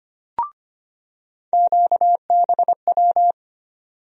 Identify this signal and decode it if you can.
Morse code, heard as a keyed tone, about 25 wpm, 700 Hz. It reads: QBW
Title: USA Call Sign Postfixes - 3 Letters 25wpm